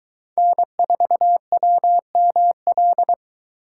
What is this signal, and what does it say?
Morse code, heard as a keyed tone, about 22 wpm, 700 Hz. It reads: N4WML